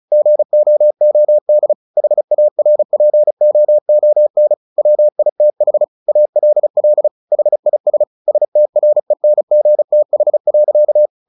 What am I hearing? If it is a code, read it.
Morse code, heard as a keyed tone, about 34 wpm, 600 Hz. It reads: GOOD HARPOON WITH ALL HIS STRENGTH.
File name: the old man and the sea 2 35wpm